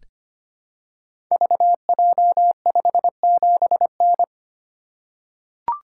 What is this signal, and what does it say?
Morse code, heard as a keyed tone, about 25 wpm, 690 Hz. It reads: VJ57N
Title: Code Groups - 5 Letter:Numbers - 25wpm